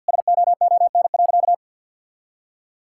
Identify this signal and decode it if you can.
Morse code, heard as a keyed tone, about 51 wpm, 700 Hz. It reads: SOON.
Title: Sentences from Top 1600 Words 50wpm